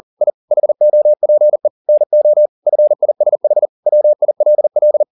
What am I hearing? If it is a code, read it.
Morse code, heard as a keyed tone, about 40 wpm, 600 Hz. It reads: I HOPE NO FISH WILL